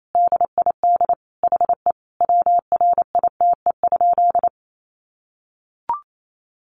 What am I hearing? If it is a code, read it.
Morse code, heard as a keyed tone, about 28 wpm, 700 Hz. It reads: DID HE WRITE?
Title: Sentences from Top 800 Words 28wpm